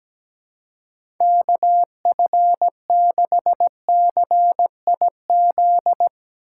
Morse code, heard as a keyed tone, about 17 wpm, 700 Hz. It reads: KF6CIZ
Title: USA Call Signs - 6 Characters - 8x Character Spacing - 17wpm